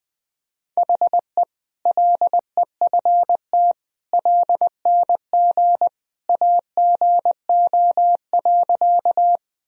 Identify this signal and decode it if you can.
Morse code, heard as a keyed tone, about 21 wpm, 700 Hz. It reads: HE LEFT LNG AGO.
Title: Sentences from Top 500 Words and Abbreviations 20wpm